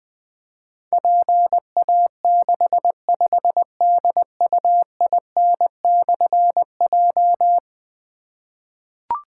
Morse code, heard as a keyed tone, about 20 wpm, 700 Hz. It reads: PA65DUIN/J
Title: Difficult Call Signs 20wpm